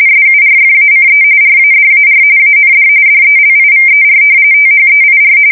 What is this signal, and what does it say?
RTTY, about 45.45 baud. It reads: SURFACE RADAR AND DELIVER THEIR
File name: fsk170 / battleshipnj